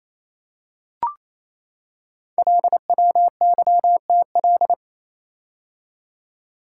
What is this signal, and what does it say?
Morse code, heard as a keyed tone, about 29 wpm, 700 Hz. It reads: LWYTL
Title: Code Groups - 5 Letters - 28wpm